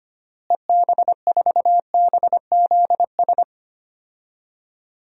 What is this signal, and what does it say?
Morse code, heard as a keyed tone, about 25 wpm, 700 Hz. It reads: EB4BZS
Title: DXCC-100 Call Signs - 6 Characters - Repeated 3x - 25wpm